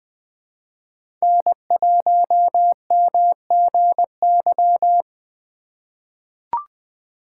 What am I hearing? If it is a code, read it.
Morse code, heard as a keyed tone, about 20 wpm, 700 Hz. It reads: N1MGY
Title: USA Call Signs - 5 Characters - 4x Character Spacing - 20wpm